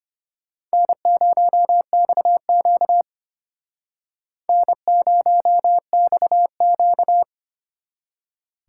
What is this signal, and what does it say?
Morse code, heard as a keyed tone, about 32 wpm, 700 Hz. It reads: N0XQ N0XQ